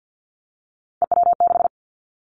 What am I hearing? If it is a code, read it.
Morse code, heard as a keyed tone, about 50 wpm, 700 Hz. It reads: EW6
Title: Callsign Prefixes for Top 50 DXCC - 50wpm